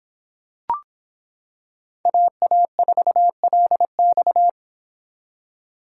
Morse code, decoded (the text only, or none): AA4LX